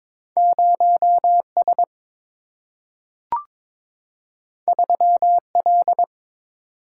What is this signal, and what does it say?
Morse code, heard as a keyed tone, about 22 wpm, 700 Hz. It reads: 0S 3L
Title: Single-Letter Single-Number 22wpm